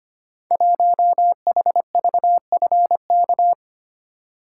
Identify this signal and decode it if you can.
Morse code, heard as a keyed tone, about 25 wpm, 700 Hz. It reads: 1HVFK
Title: Code Groups - 5 Letter:Numbers - 25wpm